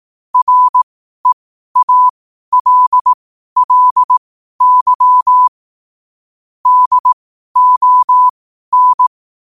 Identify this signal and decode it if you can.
Morse code, heard as a keyed tone, about 18 wpm, spaced as for 12 wpm, 1000 Hz. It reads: REALLY DON
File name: qotdfiles / 12wpm